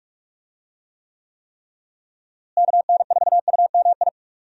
Morse code, heard as a keyed tone, about 43 wpm, 700 Hz. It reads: KN4UMI
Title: USA Call Signs - 6 Characters - Repeated 3x - 45wpm